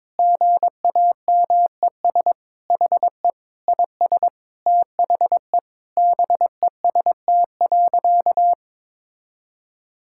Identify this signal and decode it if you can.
Morse code, heard as a keyed tone, about 22 wpm, 700 Hz. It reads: GAMES HE IS THE BEST.
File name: practice / 22wpm / Sentences with 3-Word Phrases - 22wpm